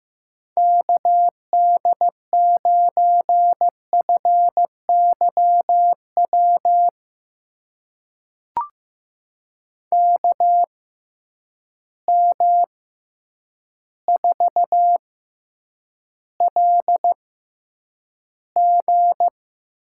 Morse code, heard as a keyed tone, about 15 wpm, spaced as for 8 wpm, 700 Hz. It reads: KD9FYW KM4LG
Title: USA Call Signs - 6 Characters - 6x Character Spacing - 15wpm